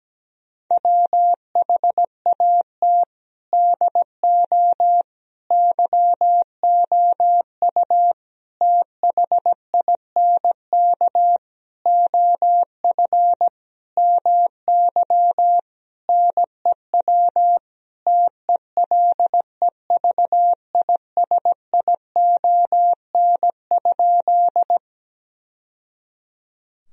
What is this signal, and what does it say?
Morse code, heard as a keyed tone, about 17 wpm, 700 Hz. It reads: WHAT DO YOU THINK OF MY NEW TELEVISION?
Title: Sentences from Top 1000 Words - Review Part 4 - 17wpm